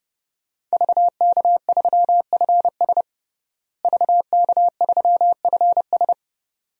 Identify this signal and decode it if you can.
Morse code, heard as a keyed tone, about 30 wpm, 700 Hz. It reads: VK3FS VK3FS
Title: USA Call Signs - 5 Characters - Repeated 2x - 30wpm